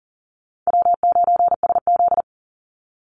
Morse code, heard as a keyed tone, about 37 wpm, 700 Hz. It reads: W9SZ
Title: Call Signs Repeated 3x - 40wpm